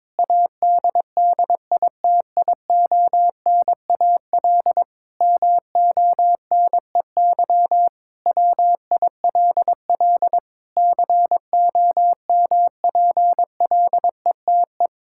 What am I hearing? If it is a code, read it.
Morse code, heard as a keyed tone, about 22 wpm, 700 Hz. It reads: ADDITIONAL MONEY WILL COMPLETE
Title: Sentences from Top 1000 Words - Review Part 5 - 22wpm